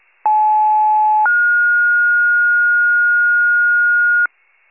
Two-tone page, heard as a one-second tone, then a three-second tone, about 860 then 1430 Hz, with noise. It is heard from 0.2 to 4.3 s.